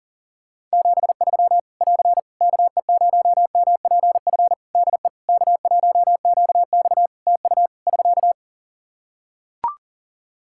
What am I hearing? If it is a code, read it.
Morse code, heard as a keyed tone, about 40 wpm, 700 Hz. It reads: 73 <AR> KE0MPF DE K1QX TU <SK>
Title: Practice QSO - Exchange 4 - Ending - 40wpm